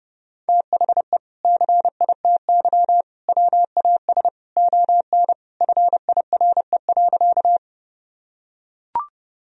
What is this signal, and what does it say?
Morse code, heard as a keyed tone, about 32 wpm, 700 Hz. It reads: THE CITY WAS ON FIRE.
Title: Sentences from Top 1000 Words - Review Part 1 - 30wpm